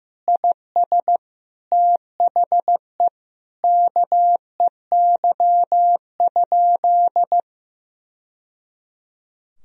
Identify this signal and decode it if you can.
Morse code, heard as a keyed tone, about 15 wpm, 700 Hz. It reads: IS THE KEY?
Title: Sentences from Top 1000 Words - Review Part 4 - 15wpm